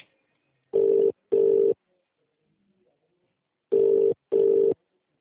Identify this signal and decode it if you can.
Ringback tone (UK), 400 + 450 Hz.